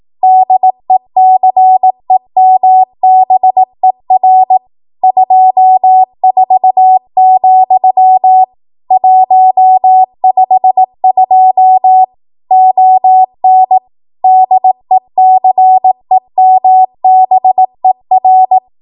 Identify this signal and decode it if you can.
Morse code, heard as a keyed tone, about 18 wpm, 750 Hz. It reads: DECEMBER 24, 152 ON DECEMBER
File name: Archive / 18 WPM / 18WPM